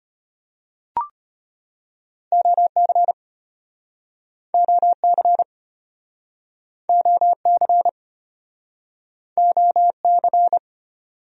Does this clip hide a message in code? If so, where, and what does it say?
Yes: Morse code from 2.3 to 10.6 s, heard as a keyed tone, about 35 wpm, 700 Hz. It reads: OC OC OC OC